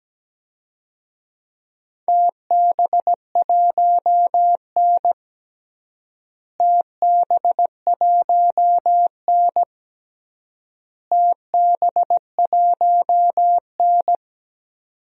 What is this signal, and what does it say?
Morse code, heard as a keyed tone, about 17 wpm, 700 Hz. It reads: TB1N TB1N TB1N
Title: DXCC-100 Call Signs - 4 Characters - Repeated 3x - 17wpm